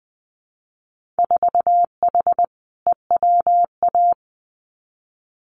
Morse code, heard as a keyed tone, about 20 wpm, 700 Hz. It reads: 4H EWA
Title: Winter Field Day Exchanges Encore 20wpm